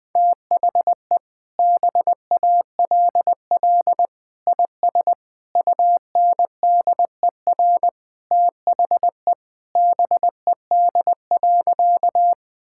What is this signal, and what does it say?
Morse code, heard as a keyed tone, about 21 wpm, 700 Hz. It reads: THE BALL IS UNDER THE BED.
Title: Sentences from Top 1000 Words - Review Part 6 - 20wpm